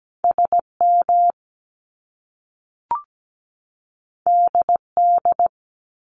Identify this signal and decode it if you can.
Morse code, heard as a keyed tone, about 17 wpm, 700 Hz. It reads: SM DD
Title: Top Two Letter Combinations in Words 17wpm